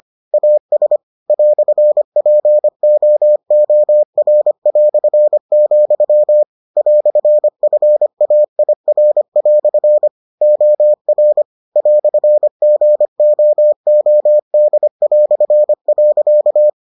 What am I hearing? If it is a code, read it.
Morse code, heard as a keyed tone, about 25 wpm, 600 Hz. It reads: AS "POOR", "FAIR" OR "GOOD".